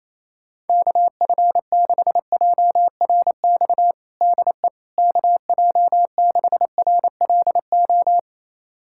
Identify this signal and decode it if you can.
Morse code, heard as a keyed tone, about 28 wpm, 700 Hz. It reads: KF6JRX DE KJ6RLO
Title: USA Callsign DE Callsign 28wpm